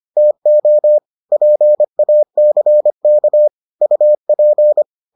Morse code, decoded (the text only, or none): TO PACK UP